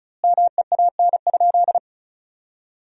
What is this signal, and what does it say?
Morse code, heard as a keyed tone, about 36 wpm, 700 Hz. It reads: MEAN?
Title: 2x Word-Spacing - Sentences from Top 200 Words - 35wpm